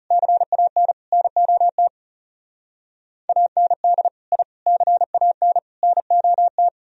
Morse code, heard as a keyed tone, about 35 wpm, 700 Hz. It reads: CAN NOT AND I CAN NOT
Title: Speed-Racing - Sets of 4 Words 30wpm